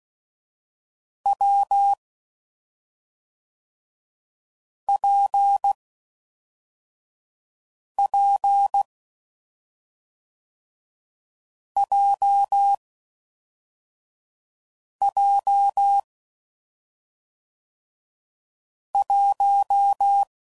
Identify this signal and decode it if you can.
Morse code, heard as a keyed tone, about 16 wpm, 800 Hz. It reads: W P P J J 1
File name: Combinations / Difficult Combos / AWJ1